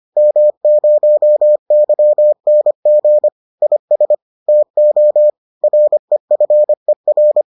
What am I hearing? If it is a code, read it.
Morse code, heard as a keyed tone, about 25 wpm, 600 Hz. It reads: M0YNG IS TO REFER